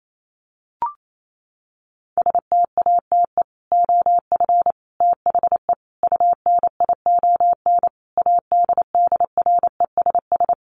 Morse code, heard as a keyed tone, about 28 wpm, 700 Hz. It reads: STATE OF THE UNION ADDRESS
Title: Sets of 5 Words Encore 28wpm